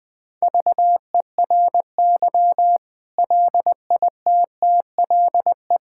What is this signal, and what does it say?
Morse code, heard as a keyed tone, about 20 wpm, 700 Hz. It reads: VERY LITTLE
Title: Sentences from Top 1000 Words - Review Part 6 - 20wpm